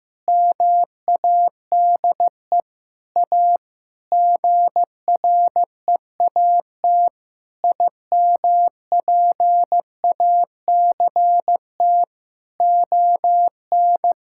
Morse code, heard as a keyed tone, about 15 wpm, 700 Hz. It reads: MADE A GREAT IMPACT ON
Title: Sentences from Top 1000 Words - Review Part 6 - 15wpm